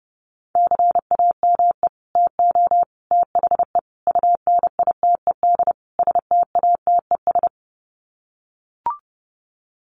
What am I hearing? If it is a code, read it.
Morse code, heard as a keyed tone, about 30 wpm, 700 Hz. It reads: CAME TO THE UNITED STATES
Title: Sets of 5 Words Encore2 30wpm